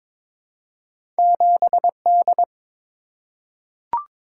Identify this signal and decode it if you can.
Morse code, heard as a keyed tone, about 22 wpm, 700 Hz. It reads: 7D
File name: practice / 22wpm / Two Letter-Number Combinations - 22wpm